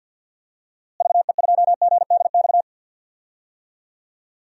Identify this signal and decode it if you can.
Morse code, heard as a keyed tone, about 49 wpm, 700 Hz. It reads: VE2GDX
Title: DXCC-100 Call Signs - 6 Characters - Repeated 3x - 50wpm